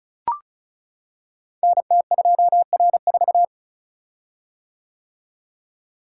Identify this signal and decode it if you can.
Morse code, heard as a keyed tone, about 36 wpm, 700 Hz. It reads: NT2R4
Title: Code Groups - 5 Letter:Numbers - 35wpm